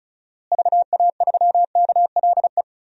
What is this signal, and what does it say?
Morse code, heard as a keyed tone, about 36 wpm, 700 Hz. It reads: VA3KLE
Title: DXCC-100 Call Signs - 6 Characters - Repeated 3x - 35wpm